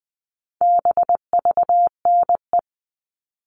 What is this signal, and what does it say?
Morse code, heard as a keyed tone, about 20 wpm, 700 Hz. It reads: BVNE
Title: All Cut Numbers - 4-Digit 20wpm